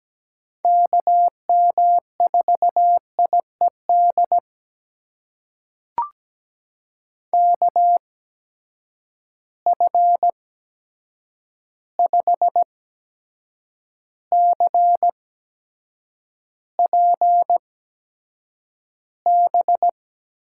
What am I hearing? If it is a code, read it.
Morse code, heard as a keyed tone, about 17 wpm, spaced as for 7 wpm, 700 Hz. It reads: KM4IED KF5CPB
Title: USA Call Signs - 6 Characters - 8x Character Spacing - 17wpm